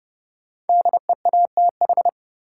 Morse code, heard as a keyed tone, about 30 wpm, 700 Hz. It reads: DEATH
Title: Sentences from Top 1100 Words - New Word Drill Alt - 30wpm